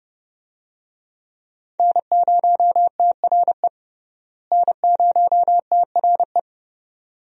Morse code, heard as a keyed tone, about 30 wpm, 700 Hz. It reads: N0TRE N0TRE
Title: USA Call Signs - 5 Characters - Repeated 2x - 30wpm